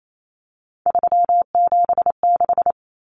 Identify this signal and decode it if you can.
Morse code, heard as a keyed tone, about 28 wpm, 700 Hz. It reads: 376
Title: Common Cut Numbers - 3-Digit 28wpm